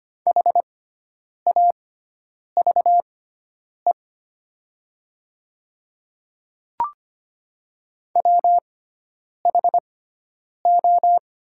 Morse code, heard as a keyed tone, about 25 wpm, spaced as for 9 wpm, 700 Hz. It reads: HAVE WHO
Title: 6x Character Spacing - Top 100 Words - 25wpm